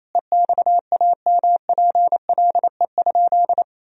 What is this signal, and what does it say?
Morse code, heard as a keyed tone, about 28 wpm, 700 Hz. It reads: EXAMPLE?
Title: Sentences from Top 1000 Words - Review Part 1 - 28wpm